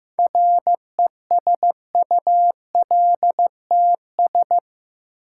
Morse code, heard as a keyed tone, about 15 wpm, 700 Hz. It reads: RESULTS
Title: Sets of 4 Words - New Word Drill - 15wpm